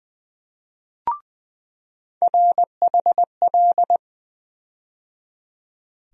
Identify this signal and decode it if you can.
Morse code, heard as a keyed tone, about 21 wpm, 700 Hz. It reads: RHL
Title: USA Call Sign Postfixes - 3 Letters 20wpm